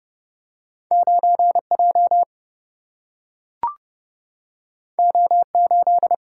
Morse code, heard as a keyed tone, about 30 wpm, 700 Hz. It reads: 9 J O 8
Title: Single-Letter Single-Number 30wpm